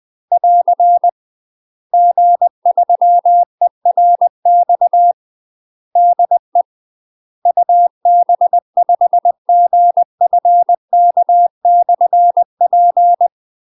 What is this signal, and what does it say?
Morse code, heard as a keyed tone, about 20 wpm, 700 Hz. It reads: <AR> G3ERX DE UB5GFK/P
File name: UK MASTER QSOs TRAINING / 20WPM / UK MASTER QSOs 0068